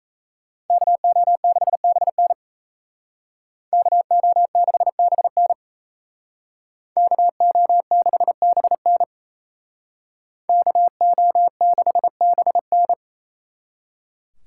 Morse code, heard as a keyed tone, about 35 wpm, 700 Hz. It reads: KO6BN KO6BN KO6BN KO6BN